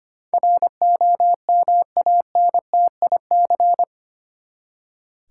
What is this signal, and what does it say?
Morse code, heard as a keyed tone, about 25 wpm, 700 Hz. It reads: ROMANTIC
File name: Eight Letter Words - 25wpm